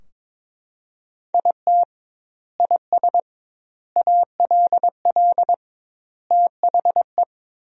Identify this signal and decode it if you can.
Morse code, heard as a keyed tone, about 22 wpm, 700 Hz. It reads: IT IS ALL THE
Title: 2x Word-Spacing - Sentences from Top 200 Words - 22wpm